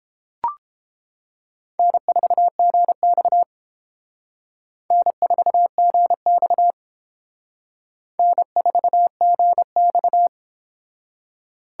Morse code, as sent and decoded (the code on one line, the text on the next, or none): -. ....- --. -..- / -. ....- --. -..- / -. ....- --. -..-
N4GX N4GX N4GX